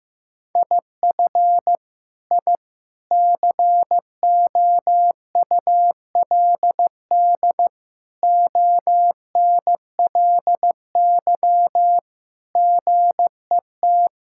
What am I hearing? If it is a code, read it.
Morse code, heard as a keyed tone, about 15 wpm, 700 Hz. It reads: IF I COULD ONLY GET
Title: Sentences from Top 1000 Words - Review Part 6 - 15wpm